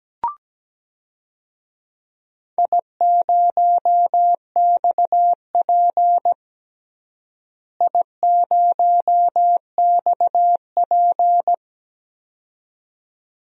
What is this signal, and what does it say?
Morse code, heard as a keyed tone, about 17 wpm, 700 Hz. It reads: I0XP I0XP
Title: DXCC-100 Call Signs - 4 Characters - Repeated 2x - 17wpm